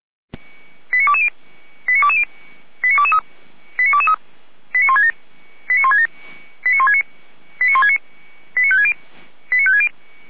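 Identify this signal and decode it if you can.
Five-tone selective call (ZVEI), heard as quick runs of five tones, tones of about 70 ms. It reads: 89220 89220 89223 89223 89167 89167 89169 89169 89560 89560
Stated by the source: ZVEI five-tone call received on SDR, from the audio server of Feuerwehr Orlamünde.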